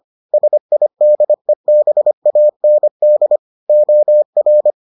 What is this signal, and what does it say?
Morse code, heard as a keyed tone, about 25 wpm, 600 Hz. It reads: SIDEBAND OR